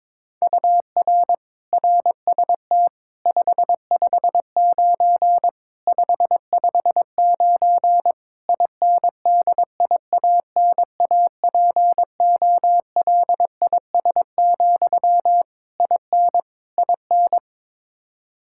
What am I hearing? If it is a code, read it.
Morse code, heard as a keyed tone, about 22 wpm, 700 Hz. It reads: UR RST 559 559 INDIANAPOLIS, IN IN
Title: Practice QSO - Exchange 1 - Beginning - 22wpm